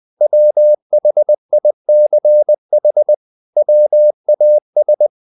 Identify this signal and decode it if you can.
Morse code, heard as a keyed tone, about 20 wpm, 600 Hz. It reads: WHICH WAS
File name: the old man and the sea / the old man and the sea part 1 20wpm